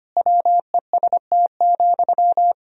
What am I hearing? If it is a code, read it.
Morse code, heard as a keyed tone, about 25 wpm, 700 Hz. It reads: WEST,
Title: Sentences from Top 1000 Words - Review Part 2 - 25wpm